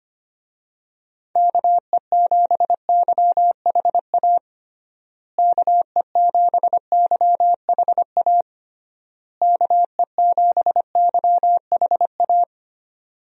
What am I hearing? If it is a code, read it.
Morse code, heard as a keyed tone, about 25 wpm, 700 Hz. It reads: KE7YHA KE7YHA KE7YHA
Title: USA Call Signs - 6 Characters - Repeated 3x - 25wpm